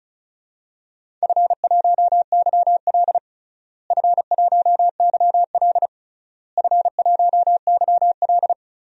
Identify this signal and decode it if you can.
Morse code, heard as a keyed tone, about 34 wpm, 700 Hz. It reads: F1YL F1YL F1YL
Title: DXCC-100 Call Signs - 4 Characters - Repeated 3x - 35wpm